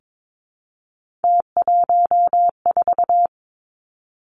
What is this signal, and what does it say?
Morse code, heard as a keyed tone, about 22 wpm, 700 Hz. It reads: T14
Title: Three Challenging Letter-Number Combinations 22wpm